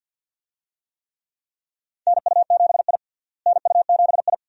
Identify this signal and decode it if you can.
Morse code, heard as a keyed tone, about 51 wpm, 700 Hz. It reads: NU7I NU7I
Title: USA Call Signs - 4 Characters - Repeated 2x - 50wpm